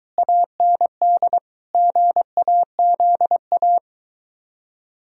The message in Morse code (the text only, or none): AND GAZA